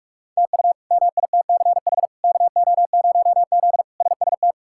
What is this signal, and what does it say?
Morse code, heard as a keyed tone, about 46 wpm, 700 Hz. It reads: TU MITCH KO0Z SST
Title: K1USN SST QSO Practice - 45wpm